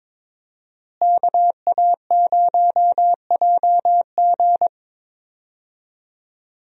Morse code, heard as a keyed tone, about 22 wpm, 700 Hz. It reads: KA0JG
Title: USA Call Signs - 5 Characters - Repeated 3x - 22wpm